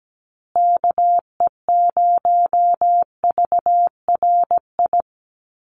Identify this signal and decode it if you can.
Morse code, heard as a keyed tone, about 17 wpm, 700 Hz. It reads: KE0VRI
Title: USA Call Signs Repeated 2x 17wpm